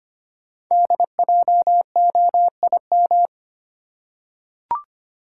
Morse code, heard as a keyed tone, about 25 wpm, 700 Hz. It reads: DJOIM
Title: Code Groups - 5 Letters - 25wpm